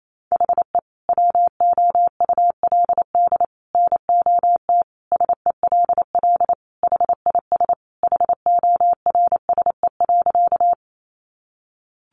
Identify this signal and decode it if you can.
Morse code, heard as a keyed tone, about 28 wpm, 700 Hz. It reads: HE WOULD NOT SELL HIS HORSE.
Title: Sentences from Top 1100 Words 28wpm